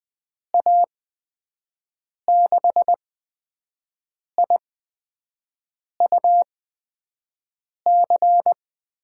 Morse code, heard as a keyed tone, about 20 wpm, spaced as for 5 wpm, 700 Hz. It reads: A6IUC
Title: USA Call Signs - 6 Characters - 8x Character Spacing - 20wpm